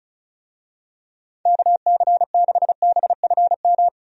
Morse code, heard as a keyed tone, about 36 wpm, 700 Hz. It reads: KC6BFM
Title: USA Call Signs - 6 Characters - 35wpm